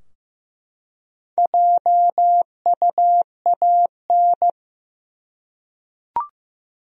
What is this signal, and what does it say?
Morse code, heard as a keyed tone, about 15 wpm, 700 Hz. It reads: JUAN
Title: USA Common Names - Max 5 Letters - 15wpm